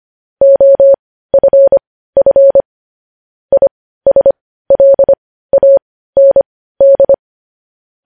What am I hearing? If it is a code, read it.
Morse code, heard as a keyed tone, about 25 wpm, spaced as for 15 wpm, 550 Hz. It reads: OFF ISLAND